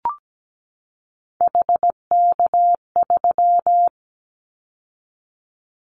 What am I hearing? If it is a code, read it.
Morse code, heard as a keyed tone, about 17 wpm, 700 Hz. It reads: HK3
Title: Callsign Prefixes for Top 100 DXCC - 17wpm